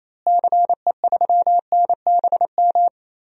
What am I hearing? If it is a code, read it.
Morse code, heard as a keyed tone, about 28 wpm, 700 Hz. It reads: CE3NBM
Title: DXCC-100 Call Signs - 6 Characters - Repeated 3x - 28wpm